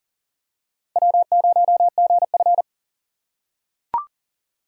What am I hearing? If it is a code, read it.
Morse code, heard as a keyed tone, about 40 wpm, 700 Hz. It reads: W0GF
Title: USA Call Signs - 4 Characters - 2x Character Spacing - 40wpm